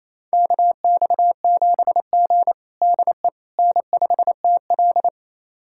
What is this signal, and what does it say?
Morse code, heard as a keyed tone, about 28 wpm, 700 Hz. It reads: KX7G DE N5TL
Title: USA Callsign DE Callsign - 4 Characters - 28wpm